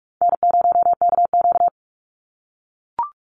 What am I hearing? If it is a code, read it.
Morse code, heard as a keyed tone, about 44 wpm, 700 Hz. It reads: N0KQ
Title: Easy Call Signs - 45wpm